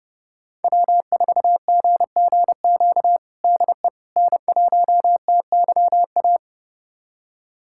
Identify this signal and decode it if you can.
Morse code, heard as a keyed tone, about 32 wpm, 700 Hz. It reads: W4GGQ DE N1TYA
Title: USA Callsign DE Callsign - 5 Characters - 30wpm